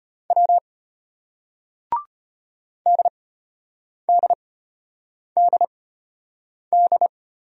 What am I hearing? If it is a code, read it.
Morse code, heard as a keyed tone, about 34 wpm, spaced as for 8 wpm, 700 Hz. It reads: W DDDD